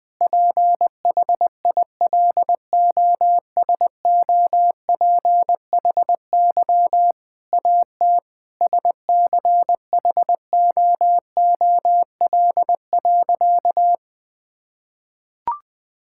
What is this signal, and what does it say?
Morse code, heard as a keyed tone, about 20 wpm, 700 Hz. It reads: PHILOSOPHY AT SCHOOL.
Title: Sentences from Top 1000 Words - Review Part 6 - 20wpm